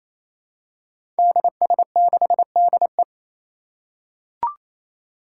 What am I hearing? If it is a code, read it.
Morse code, heard as a keyed tone, about 27 wpm, 700 Hz. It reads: DS6DE
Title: Five Challenging Letter-Number Combinations 28wpm